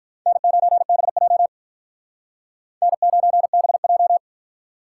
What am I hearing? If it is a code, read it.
Morse code, heard as a keyed tone, about 47 wpm, 700 Hz. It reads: N9BJ N9BJ